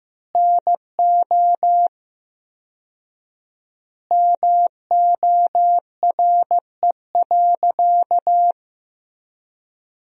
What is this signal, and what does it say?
Morse code, heard as a keyed tone, about 15 wpm, 700 Hz. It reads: NO MORE.
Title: 4x Word-Spacing - Sentences from Top 100 Words - 15wpm